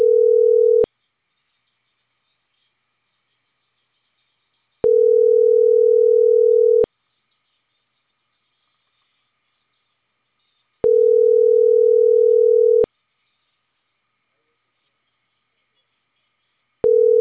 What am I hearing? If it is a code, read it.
Ringback tone (North America), 440 + 480 Hz.